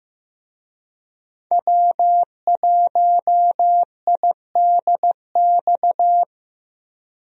Morse code, heard as a keyed tone, about 15 wpm, 700 Hz. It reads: W1IDX
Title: USA Call Signs - 5 Characters - Repeated 2x - 15wpm